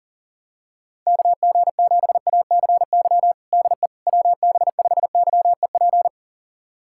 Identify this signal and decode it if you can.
Morse code, heard as a keyed tone, about 40 wpm, 700 Hz. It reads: KG7ACY DE WB5YEP
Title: USA Callsign DE Callsign - 6 Characters - 40wpm